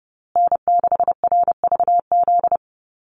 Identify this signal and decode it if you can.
Morse code, heard as a keyed tone, about 30 wpm, 700 Hz. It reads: N6RVZ
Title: USA Call Signs Repeated 3x 30wpm